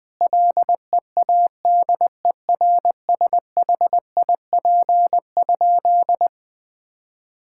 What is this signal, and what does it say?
Morse code, heard as a keyed tone, about 20 wpm, 700 Hz. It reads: LEADERSHIP?
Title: Sentences from Top 1600 Words 20wpm